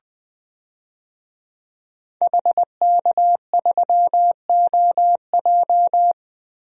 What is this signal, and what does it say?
Morse code, heard as a keyed tone, about 20 wpm, 700 Hz. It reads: HK3OJ
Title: DXCC-100 Call Signs - 5 Characters - 20wpm